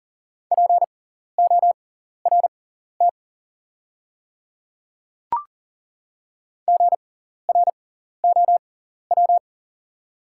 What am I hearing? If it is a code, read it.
Morse code, heard as a keyed tone, about 40 wpm, spaced as for 14 wpm, 700 Hz. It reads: PORT GROW
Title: 6x Character Spacing - Top 300 Words - 40wpm